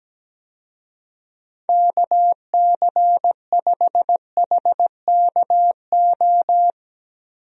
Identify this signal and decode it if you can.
Morse code, heard as a keyed tone, about 17 wpm, 700 Hz. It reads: KC5HKO
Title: USA Call Signs - 6 Characters - 17wpm